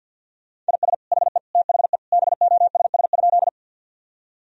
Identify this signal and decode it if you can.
Morse code, heard as a keyed tone, about 50 wpm, 700 Hz. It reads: IS HE THE BOSS?
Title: Sentences from Top 1000 Words - Review Part 4 - 50wpm